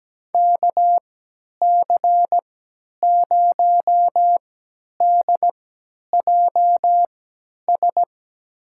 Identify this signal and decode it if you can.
Morse code, heard as a keyed tone, about 17 wpm, spaced as for 10 wpm, 700 Hz. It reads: KC0DJS
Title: USA Call Signs - 6 Characters - 3x Character Spacing - 17wpm